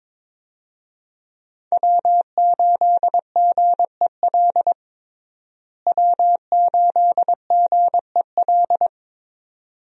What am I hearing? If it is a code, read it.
Morse code, heard as a keyed tone, about 22 wpm, 700 Hz. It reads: W8GEL W8GEL
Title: USA Call Signs - 5 Characters - Repeated 2x - 22wpm